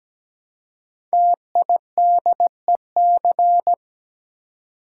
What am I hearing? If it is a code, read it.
Morse code, heard as a keyed tone, about 17 wpm, 700 Hz. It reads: TIDEC
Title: Code Groups - 5 Letters - 17wpm